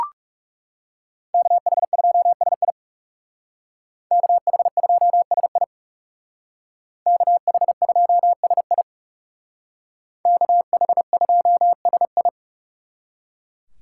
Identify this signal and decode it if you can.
Morse code, heard as a keyed tone, about 36 wpm, 700 Hz. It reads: KH2SI KH2SI KH2SI KH2SI
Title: Speed Racing - USA 2x2 Call Signs 30wpm